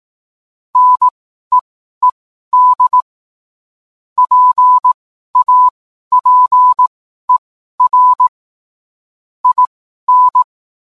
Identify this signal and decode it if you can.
Morse code, heard as a keyed tone, about 18 wpm, spaced as for 12 wpm, 1000 Hz. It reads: NEED PAPER IN